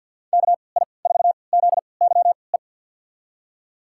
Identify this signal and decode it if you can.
Morse code, heard as a keyed tone, about 47 wpm, spaced as for 28 wpm, 700 Hz. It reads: KI4ZYE
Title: USA Call Signs - 3x Character Spacing - 50wpm